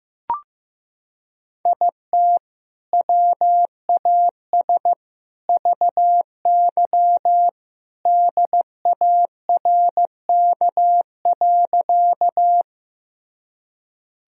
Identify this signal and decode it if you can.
Morse code, heard as a keyed tone, about 15 wpm, 700 Hz. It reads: IT WAS VY DARK.
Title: Sentences from Top 500 Words and Abbreviations 15wpm